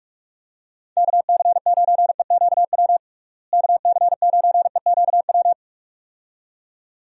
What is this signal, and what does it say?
Morse code, heard as a keyed tone, about 41 wpm, 700 Hz. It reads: KC9EQW KC9EQW
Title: USA Call Signs - 6 Characters - Repeated 2x - 45wpm